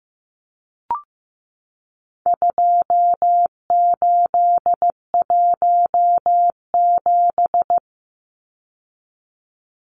Morse code, heard as a keyed tone, about 15 wpm, 700 Hz. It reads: 2817
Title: Numbers - 4-Digit 15wpm